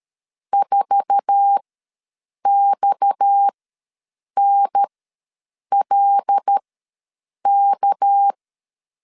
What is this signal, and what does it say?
Morse code, heard as a keyed tone, about 13 wpm, spaced as for 7 wpm, 800 Hz. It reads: 4XNLK
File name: LEZIONI / 8 YJ45 / 8 YJ4 ws 12wpm 3249